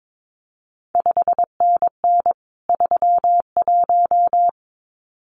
Morse code, heard as a keyed tone, about 22 wpm, 700 Hz. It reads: 5NN 31
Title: CQ WW DX Contest Exchange - 22wpm